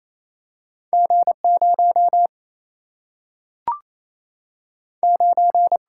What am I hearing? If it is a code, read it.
Morse code, heard as a keyed tone, about 28 wpm, 700 Hz. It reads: G 0 9
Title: Single-Letter Single-Number 28wpm